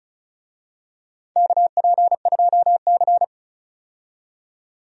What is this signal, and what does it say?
Morse code, heard as a keyed tone, about 34 wpm, 700 Hz. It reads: KP2C
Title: DXCC-100 Call Signs - 4 Characters - 35wpm